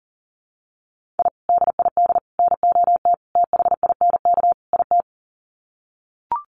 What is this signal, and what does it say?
Morse code, heard as a keyed tone, about 40 wpm, 700 Hz. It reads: I DID NOT THINK IT
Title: Sets of 5 Words Encore2 40wpm